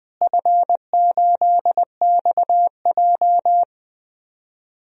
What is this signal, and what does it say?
Morse code, heard as a keyed tone, about 20 wpm, 700 Hz. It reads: F8XJ
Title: DXCC-100 Call Signs - 4 Characters - Repeated 2x - 20wpm